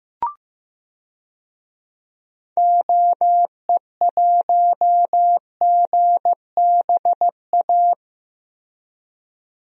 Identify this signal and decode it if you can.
Morse code, heard as a keyed tone, about 15 wpm, 700 Hz. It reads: OE1GBA
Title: DXCC-100 Call Signs - 6 Characters - Repeated 3x - 15wpm